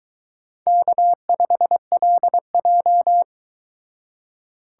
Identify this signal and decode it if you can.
Morse code, heard as a keyed tone, about 22 wpm, 700 Hz. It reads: K5LJ